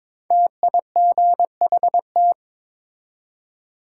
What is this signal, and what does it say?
Morse code, heard as a keyed tone, about 23 wpm, 700 Hz. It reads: TIGHT
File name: 5 Letter Words - First 3 Letters Given - 22wpm